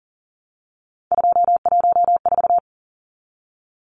Morse code, heard as a keyed tone, about 40 wpm, 700 Hz. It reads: 214
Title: Numbers - 3-Digit 40wpm